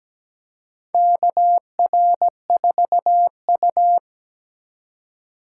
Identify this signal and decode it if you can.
Morse code, heard as a keyed tone, about 17 wpm, 700 Hz. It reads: KR4U
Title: USA Call Signs - 4 Characters - 4x Character Spacing - 17wpm